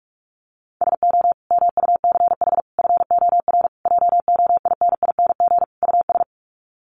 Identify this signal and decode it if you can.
Morse code, heard as a keyed tone, about 44 wpm, 700 Hz. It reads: SO MUCH FOR JOINING US
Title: Sets of 5 Words 45wpm